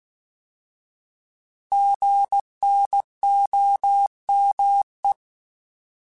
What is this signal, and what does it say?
Morse code, heard as a keyed tone, about 16 wpm, 800 Hz. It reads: GNOME
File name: Alphabetical / Alphabetical mp3s / Gs @15